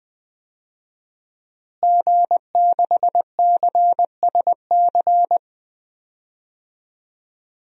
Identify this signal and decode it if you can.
Morse code, heard as a keyed tone, about 20 wpm, 700 Hz. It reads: G6CSC
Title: DXCC-100 Call Signs - 5 Characters - Repeated 3x - 20wpm